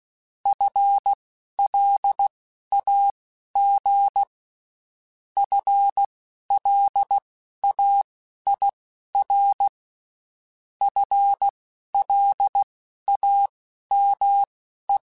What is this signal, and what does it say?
Morse code, heard as a keyed tone, about 16 wpm, spaced as for 11 wpm, 800 Hz. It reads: FLAG FLAIR FLAME
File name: Spaced / FL Runs sp mp3s / FL Runs sp @15